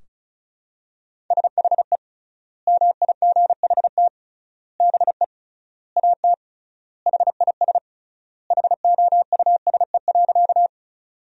Morse code, heard as a keyed tone, about 34 wpm, 700 Hz. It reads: SHE MIGHT BE AT HIS HOUSE.
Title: 3x Word-Spacing - Sentences from Top 300 Words - 35wpm